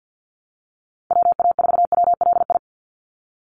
Morse code, heard as a keyed tone, about 50 wpm, 700 Hz. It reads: WA4WLI
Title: Call Signs - 50wpm